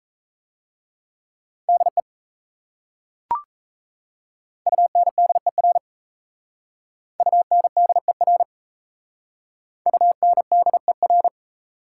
Morse code, heard as a keyed tone, about 37 wpm, 700 Hz. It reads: DE UNDER UNDER UNDER